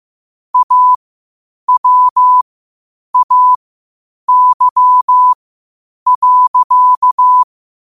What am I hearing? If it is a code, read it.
Morse code, heard as a keyed tone, about 15 wpm, spaced as for 8 wpm, 1000 Hz. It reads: AWAY.
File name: qotdfiles / 8wpm